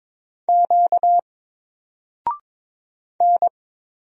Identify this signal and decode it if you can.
Morse code, heard as a keyed tone, about 22 wpm, 700 Hz. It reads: Q N
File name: Single-Letter-Number-22wpm